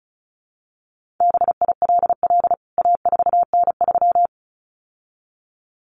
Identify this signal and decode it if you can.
Morse code, heard as a keyed tone, about 34 wpm, 700 Hz. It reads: BILL A4N3
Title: CWT Exchange 35wpm